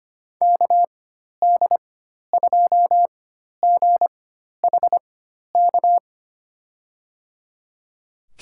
Morse code, heard as a keyed tone, about 25 wpm, spaced as for 12 wpm, 700 Hz. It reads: KD2GHK
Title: USA Call Signs - 4x Character Spacing - 25wpm